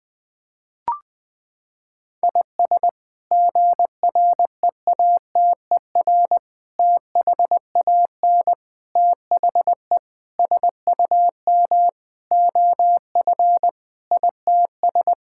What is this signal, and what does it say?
Morse code, heard as a keyed tone, about 21 wpm, 700 Hz. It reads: IS GREATER THAN THE SUM OF ITS
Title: Sets of 8 Words - 20wpm